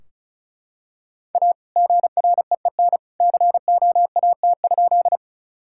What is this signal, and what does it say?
Morse code, heard as a keyed tone, about 36 wpm, 700 Hz. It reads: A GREEN COAT?
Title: Sentences from Top 1600 Words 35wpm